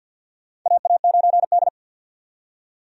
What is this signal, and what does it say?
Morse code, heard as a keyed tone, about 46 wpm, 700 Hz. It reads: AA9D
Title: USA Call Signs - 4 Characters - 50wpm